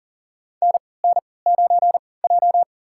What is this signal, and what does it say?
Morse code, heard as a keyed tone, about 40 wpm, spaced as for 23 wpm, 700 Hz. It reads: NN9J
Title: USA Call Signs - 3x Character Spacing - 40wpm